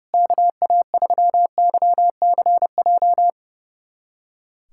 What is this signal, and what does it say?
Morse code, heard as a keyed tone, about 32 wpm, 690 Hz. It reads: KA3YCJ
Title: USA Callsign DE Callsign 30wpm